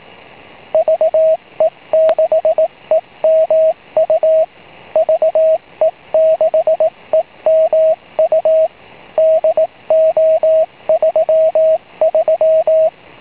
Morse code, heard as a keyed tone, about 18 wpm, 640 Hz, with radio noise. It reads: VE6EMU VE6EMU DO33
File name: va6dx / ve6emu6m